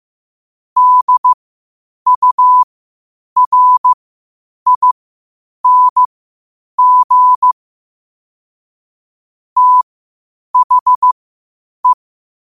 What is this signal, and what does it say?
Morse code, heard as a keyed tone, about 15 wpm, spaced as for 8 wpm, 1000 Hz. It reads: DURING THE
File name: qotdfiles / 8wpm